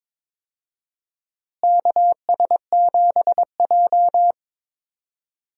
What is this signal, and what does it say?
Morse code, heard as a keyed tone, about 22 wpm, 700 Hz. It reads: KS7J
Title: USA Call Signs - 4 Characters - Repeated 2x - 22wpm